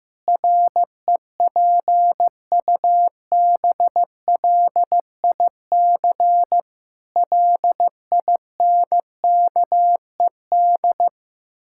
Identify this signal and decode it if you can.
Morse code, heard as a keyed tone, about 15 wpm, 700 Hz. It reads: REPUBLIC LINKED
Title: Pangram Challenge v2 - 15wpm